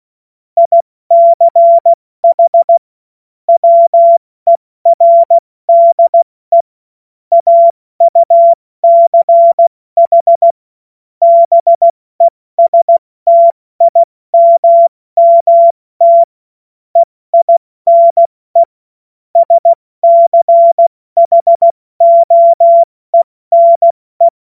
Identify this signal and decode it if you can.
Morse code, heard as a keyed tone, about 16 wpm, spaced as for 14 wpm, 680 Hz. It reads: ICH WERDE AUCH BESTIMMT EINE SCHOENE
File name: Morsekurs / Witze / Witz 05 Dezember 2023 70 F80